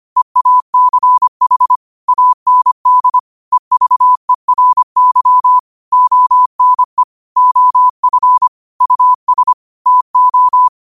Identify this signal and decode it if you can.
Morse code, heard as a keyed tone, about 25 wpm, 1000 Hz. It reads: EACH AND EVERY ONE OF US TO